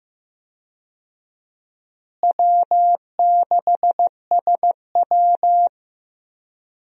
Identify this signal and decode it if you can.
Morse code, heard as a keyed tone, about 15 wpm, 700 Hz. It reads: W6SW
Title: USA Call Signs - 4 Characters - Repeated 2x - 15wpm